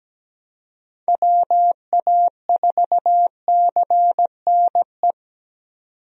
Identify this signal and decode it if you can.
Morse code, heard as a keyed tone, about 17 wpm, 700 Hz. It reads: WA4CNE
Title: USA Call Signs - 6 Characters - 17wpm